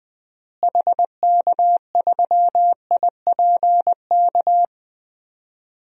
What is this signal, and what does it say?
Morse code, heard as a keyed tone, about 21 wpm, 700 Hz. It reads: HK3IPK
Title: DXCC-100 Call Signs - 6 Characters - Repeated 3x - 20wpm